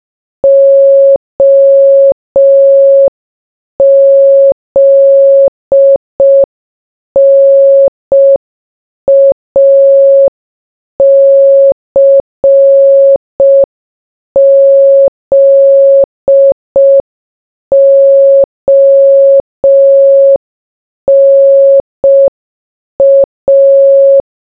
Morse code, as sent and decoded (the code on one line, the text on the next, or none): --- --.. -. .- -.-. --.. --- -. .-
OZNACZONA